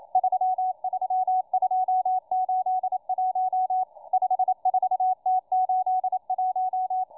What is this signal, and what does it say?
Morse code, heard as a keyed tone, about 28 wpm, 740 Hz, with radio noise. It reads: 33281 54T81